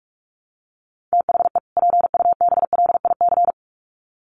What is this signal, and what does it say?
Morse code, heard as a keyed tone, about 44 wpm, 700 Hz. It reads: THE PUBLIC
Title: Sets of 2 Words 45wpm